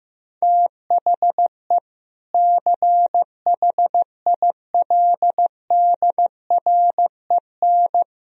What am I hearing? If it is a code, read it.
Morse code, heard as a keyed tone, about 15 wpm, 700 Hz. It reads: THE CHILDREN
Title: Sentences from Top 1000 Words - Review Part 4 - 15wpm